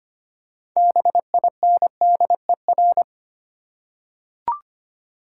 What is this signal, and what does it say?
Morse code, heard as a keyed tone, about 25 wpm, 700 Hz. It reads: BINDER
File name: practice / 25wpm / Difficult Words with Hints - 25wpm